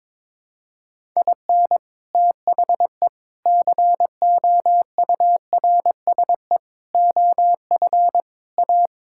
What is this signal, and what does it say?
Morse code, heard as a keyed tone, about 22 wpm, 700 Hz. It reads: IN THE COURSE OF A
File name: practice / 22wpm / Sentences with 3-Word Phrases - 22wpm